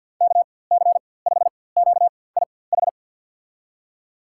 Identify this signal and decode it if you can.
Morse code, heard as a keyed tone, about 47 wpm, spaced as for 23 wpm, 700 Hz. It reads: KC5QIH
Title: USA Call Signs - 6 Characters - 4x Character Spacing - 50wpm